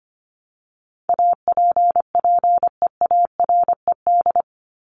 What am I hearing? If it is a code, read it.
Morse code, heard as a keyed tone, about 25 wpm, 700 Hz. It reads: APPEARED
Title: ED Suffix - Top 100 Words with Base Words 25wpm